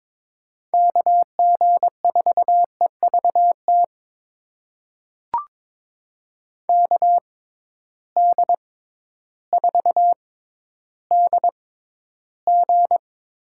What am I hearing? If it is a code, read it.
Morse code, heard as a keyed tone, about 22 wpm, spaced as for 12 wpm, 700 Hz. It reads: KG4EVT KD4DG
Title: USA Call Signs - 6x Character Spacing - 22wpm